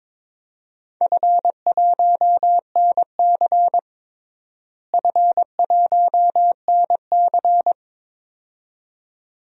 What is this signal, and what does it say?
Morse code, heard as a keyed tone, about 22 wpm, 700 Hz. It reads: F1NC F1NC
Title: DXCC-100 Call Signs - 4 Characters - Repeated 2x - 22wpm